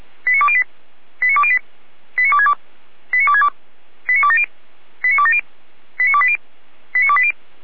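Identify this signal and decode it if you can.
Five-tone selective call (ZVEI), heard as quick runs of five tones, tones of about 70 ms. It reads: 89208 89208 89262 89262 89270 89270 89280 89280
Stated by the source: ZVEI five-tone selective call received on SDR.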